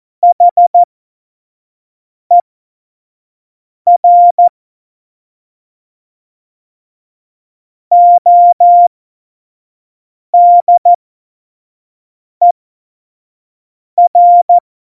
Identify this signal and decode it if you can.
Morse code, heard as a keyed tone, about 14 wpm, spaced as for 5 wpm, 700 Hz. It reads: HER ODER
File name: Morsekurs / Witze / Witz 24. Oktober 17, Tempo 25